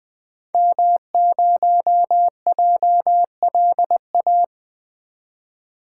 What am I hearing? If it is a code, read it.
Morse code, heard as a keyed tone, about 20 wpm, 700 Hz. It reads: M0JLA
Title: 2023 SOTA Activator Call Signs - 20wpm